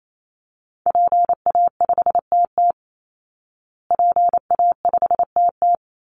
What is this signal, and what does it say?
Morse code, heard as a keyed tone, about 28 wpm, 700 Hz. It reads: PA5TT PA5TT
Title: Call Signs Repeated 2x - 28wpm